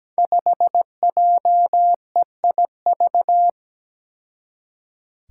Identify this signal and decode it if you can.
Morse code, heard as a keyed tone, about 17 wpm, 700 Hz. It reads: 5JEIV
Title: Five Challenging Letter-Number Combinations 17wpm